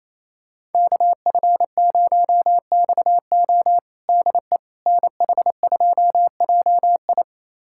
Morse code, heard as a keyed tone, about 28 wpm, 700 Hz. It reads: KF0XO DE NH2JI
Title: USA Callsign DE Callsign - 5 Characters - 28wpm